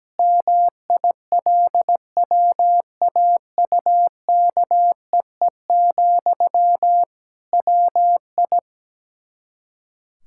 Morse code, heard as a keyed tone, about 17 wpm, 700 Hz. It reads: MILWAUKEE, WI
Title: K1USN SST QSO Practice 17wpm